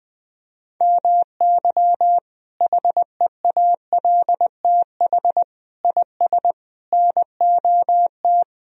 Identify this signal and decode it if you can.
Morse code, heard as a keyed tone, about 21 wpm, 700 Hz. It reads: MY HEALTH IS NOT
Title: Sentences from Top 1000 Words - Review Part 3 - 20wpm